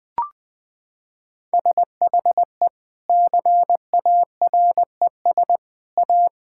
Morse code, heard as a keyed tone, about 20 wpm, 700 Hz. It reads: SHE CARES A
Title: Sentences with 3-Word Phrases - 20wpm